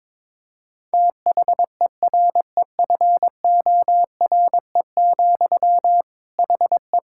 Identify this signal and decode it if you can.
Morse code, heard as a keyed tone, about 22 wpm, 700 Hz. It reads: THEREFORE, HE
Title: Sentences from Top 1000 Words - Review Part 5 - 22wpm